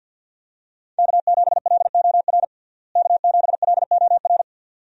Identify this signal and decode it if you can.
Morse code, heard as a keyed tone, about 52 wpm, 700 Hz. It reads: K7LOR K7LOR
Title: USA Call Signs - 5 Characters - Repeated 2x - 50wpm